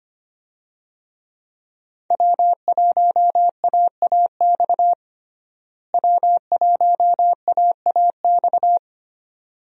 Morse code, heard as a keyed tone, about 25 wpm, 700 Hz. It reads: W1AAX W1AAX
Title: USA Call Signs - 5 Characters - Repeated 2x - 25wpm